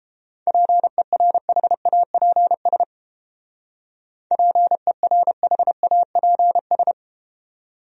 Morse code, heard as a keyed tone, about 31 wpm, 700 Hz. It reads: PERHAPS PERHAPS